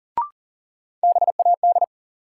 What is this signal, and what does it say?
Morse code, heard as a keyed tone, about 40 wpm, 700 Hz. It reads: BAD
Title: Sentences from Top 700 Words - New Word Drill - 40wpm